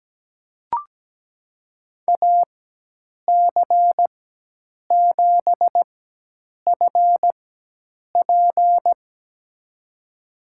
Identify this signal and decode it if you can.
Morse code, heard as a keyed tone, about 17 wpm, spaced as for 8 wpm, 700 Hz. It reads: AC7FP
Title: USA Call Signs - 5 Characters - 4x Character Spacing - 17wpm